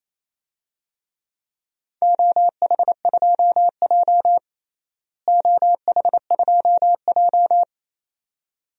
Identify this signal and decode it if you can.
Morse code, heard as a keyed tone, about 28 wpm, 700 Hz. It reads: OH2J OH2J
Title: DXCC-100 Call Signs - 4 Characters - Repeated 2x - 28wpm